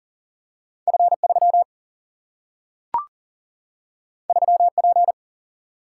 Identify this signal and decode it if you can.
Morse code, heard as a keyed tone, about 40 wpm, 700 Hz. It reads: F 3 3 P
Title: Single-Letter Single-Number 40wpm